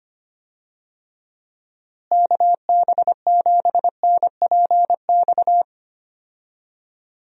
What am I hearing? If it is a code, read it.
Morse code, heard as a keyed tone, about 25 wpm, 700 Hz. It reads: KB7NPX
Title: USA Call Signs - 6 Characters - Repeated 3x - 25wpm